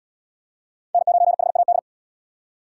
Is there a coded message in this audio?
Yes: Morse code from 0.9 to 1.8 s, heard as a keyed tone, about 71 wpm, 700 Hz. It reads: N9HAD